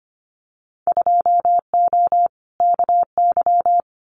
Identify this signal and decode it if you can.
Morse code, heard as a keyed tone, about 25 wpm, 700 Hz. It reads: 2O KY
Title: Winter Field Day Exchanges 25wpm